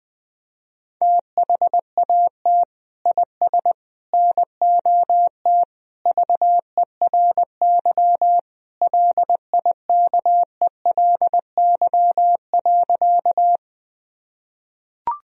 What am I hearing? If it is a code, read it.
Morse code, heard as a keyed tone, about 20 wpm, 700 Hz. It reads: THAT IS NOT VERY LIKELY.
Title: Sentences from Top 1000 Words - Review Part 4 - 20wpm